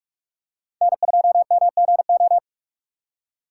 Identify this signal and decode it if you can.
Morse code, heard as a keyed tone, about 48 wpm, 690 Hz. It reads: N2MGO
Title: USA Call Signs - 4x Character Spacing - 45wpm